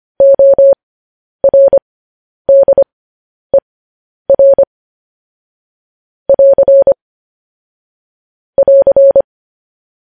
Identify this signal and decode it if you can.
Morse code, heard as a keyed tone, about 25 wpm, spaced as for 10 wpm, 550 Hz. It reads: ORDER <AR> <AR>